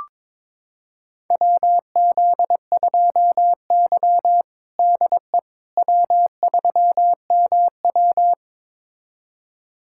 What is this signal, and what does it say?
Morse code, heard as a keyed tone, about 22 wpm, 700 Hz. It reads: WZ2Y DE W3MW
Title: USA Callsign DE Callsign - 4 Characters - 22wpm